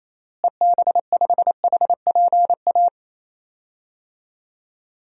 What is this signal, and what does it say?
Morse code, heard as a keyed tone, about 28 wpm, 700 Hz. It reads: EB5HPA
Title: DXCC-100 Call Signs - 6 Characters - Repeated 3x - 28wpm